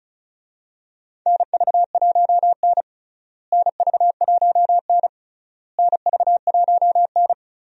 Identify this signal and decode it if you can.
Morse code, heard as a keyed tone, about 36 wpm, 700 Hz. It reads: NV1N NV1N NV1N
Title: USA Call Signs - 4 Characters - Repeated 3x - 35wpm